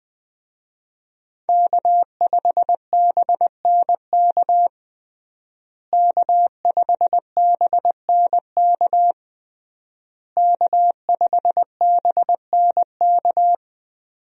Morse code, heard as a keyed tone, about 20 wpm, 700 Hz. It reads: K5BNK K5BNK K5BNK
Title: USA Call Signs - 5 Characters - Repeated 3x - 20wpm